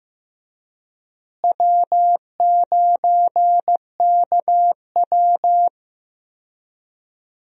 Morse code, heard as a keyed tone, about 15 wpm, 700 Hz. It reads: W9KW
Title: USA Call Signs - 4 Characters - Repeated 3x - 15wpm